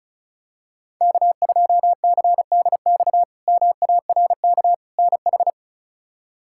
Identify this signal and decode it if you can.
Morse code, heard as a keyed tone, about 36 wpm, 700 Hz. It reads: K2CDX MARK NH
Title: K1USN SST QSO Practice - 35wpm